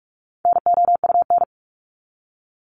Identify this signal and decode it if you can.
Morse code, heard as a keyed tone, about 44 wpm, 700 Hz. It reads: NOUN
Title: Top 500 Words 45wpm